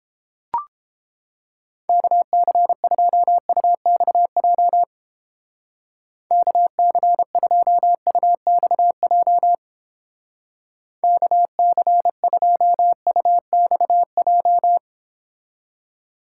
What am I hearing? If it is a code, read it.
Morse code, heard as a keyed tone, about 29 wpm, 700 Hz. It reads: KC2UXJ KC2UXJ KC2UXJ